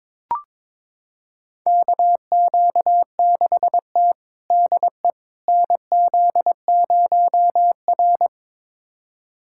Morse code, heard as a keyed tone, about 22 wpm, 700 Hz. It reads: KQ6T DE NZ0R
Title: USA Callsign DE Callsign - 4 Characters - 22wpm